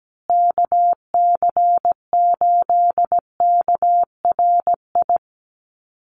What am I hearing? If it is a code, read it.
Morse code, heard as a keyed tone, about 17 wpm, 700 Hz. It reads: KC8KRI
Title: USA Call Signs Repeated 3x 17wpm